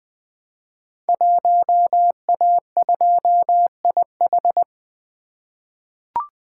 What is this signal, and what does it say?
Morse code, heard as a keyed tone, about 20 wpm, 700 Hz. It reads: 1A2IH
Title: Code Groups - 5 Letter:Numbers - 20wpm